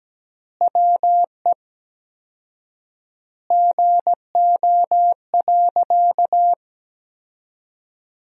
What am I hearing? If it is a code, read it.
Morse code, heard as a keyed tone, about 17 wpm, 700 Hz. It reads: WE GO.
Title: 4x Word-Spacing - Sentences from Top 200 Words - 17wpm